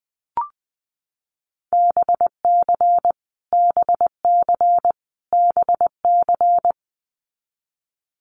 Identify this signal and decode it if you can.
Morse code, heard as a keyed tone, about 20 wpm, 700 Hz. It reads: BC BC BC
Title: Canadian Provinces and Territories - Repeated 3x - 20wpm